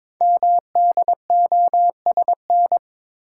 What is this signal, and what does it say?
Morse code, heard as a keyed tone, about 22 wpm, 700 Hz. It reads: MDOSN
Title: Code Groups - 5 Letter:Numbers - 22wpm